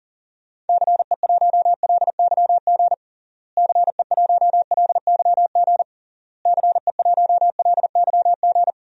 Morse code, heard as a keyed tone, about 40 wpm, 700 Hz. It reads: CE1LYG CE1LYG CE1LYG
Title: DXCC-100 Call Signs - 6 Characters - Repeated 3x - 40wpm